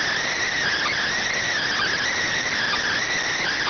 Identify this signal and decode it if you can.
Emergency Alert System (SAME) tones, heard as harsh buzzing bursts. It is heard from 0.6 to 3.2 s.